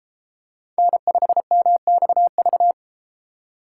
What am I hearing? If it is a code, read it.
Morse code, heard as a keyed tone, about 33 wpm, 700 Hz. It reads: N5MXV